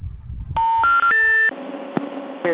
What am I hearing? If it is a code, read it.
Special information tone, with noise.